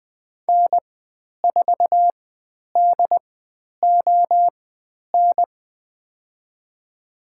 Morse code, heard as a keyed tone, about 20 wpm, spaced as for 10 wpm, 700 Hz. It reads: N4DON
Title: 102 Lesson 17g - Letter u - Call Signs 20wpm